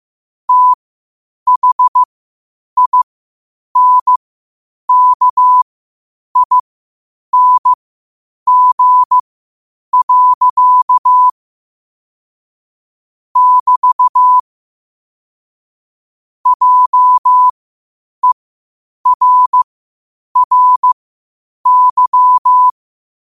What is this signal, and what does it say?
Morse code, heard as a keyed tone, about 15 wpm, spaced as for 8 wpm, 1000 Hz. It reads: THINKING. <BT> JERRY
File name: qotdfiles / 8wpm